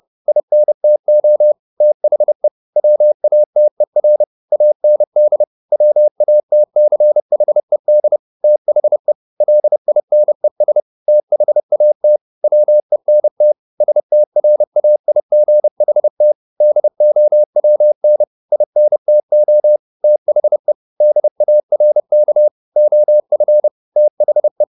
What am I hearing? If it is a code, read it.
Morse code, heard as a keyed tone, about 30 wpm, 600 Hz. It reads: INTO THE WATER AND WATCHED THE LINES THAT WENT STRAIGHT DOWN INTO THE DARK OF THE
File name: the old man and the sea / the old man and the sea 1 30wpm